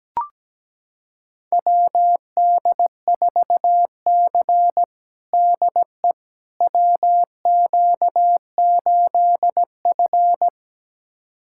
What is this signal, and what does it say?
Morse code, heard as a keyed tone, about 17 wpm, 700 Hz. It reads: WD4C DE WQ8F
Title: USA Callsign DE Callsign - 4 Characters - 17wpm